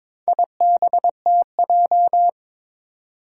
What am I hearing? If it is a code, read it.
Morse code, heard as a keyed tone, about 22 wpm, 700 Hz. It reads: IBTJ
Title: Four Challenging Letter-Number Combinations 22wpm